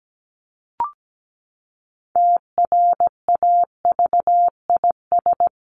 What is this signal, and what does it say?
Morse code, heard as a keyed tone, about 17 wpm, 700 Hz. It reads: TRAVIS
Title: A Sound Of Thunder 17wpm